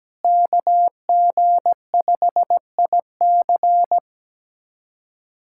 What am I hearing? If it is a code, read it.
Morse code, heard as a keyed tone, about 17 wpm, 700 Hz. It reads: KG5IC
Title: USA Callsign DE Callsign - 5 Characters - 17wpm